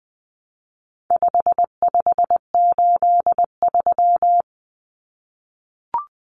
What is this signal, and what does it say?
Morse code, heard as a keyed tone, about 20 wpm, 700 Hz. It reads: 5583
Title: Common Cut Numbers - 4-Digit 20wpm